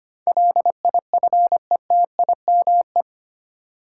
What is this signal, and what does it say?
Morse code, heard as a keyed tone, about 25 wpm, 700 Hz. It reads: LIFETIME
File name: Eight Letter Words - 25wpm